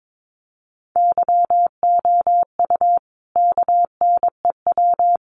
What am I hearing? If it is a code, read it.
Morse code, heard as a keyed tone, about 22 wpm, 700 Hz. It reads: YOU KNEW
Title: Sentences from Top 500 Words 22wpm